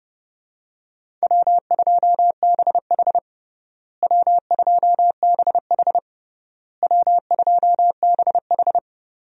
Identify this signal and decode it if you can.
Morse code, heard as a keyed tone, about 30 wpm, 700 Hz. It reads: W2BH W2BH W2BH
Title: USA Call Signs - 4 Characters - Repeated 3x - 30wpm